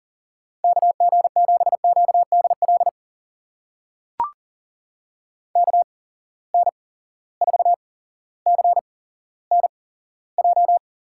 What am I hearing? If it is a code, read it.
Morse code, heard as a keyed tone, about 40 wpm, spaced as for 17 wpm, 700 Hz. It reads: KG7QDL KN4CNJ
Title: USA Call Signs - 6 Characters - 8x Character Spacing - 40wpm